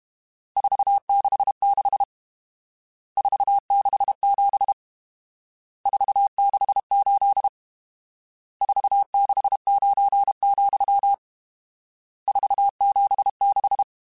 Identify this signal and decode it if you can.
Morse code, heard as a keyed tone, about 32 wpm, 800 Hz. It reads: 466 467 468 469, 476
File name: RSTs / RST-mp3s / Various RST @30